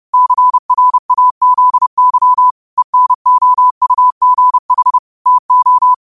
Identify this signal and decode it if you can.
Morse code, heard as a keyed tone, about 30 wpm, 1000 Hz. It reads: CRAZY ENOUGH TO
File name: qotdfiles / 30wpm